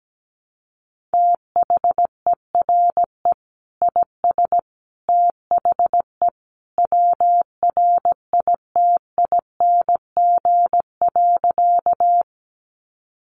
Morse code, heard as a keyed tone, about 17 wpm, 700 Hz. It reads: THERE IS THE WRITING.
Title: Sentences from Top 800 Words 17wpm